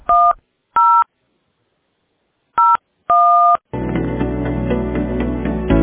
DTMF, heard as telephone touch tones, tones of about 270 ms. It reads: 10 01